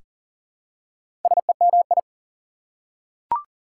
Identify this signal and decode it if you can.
Morse code, heard as a keyed tone, about 40 wpm, 700 Hz. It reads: SEMI
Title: Prefixes and Suffixes 40wpm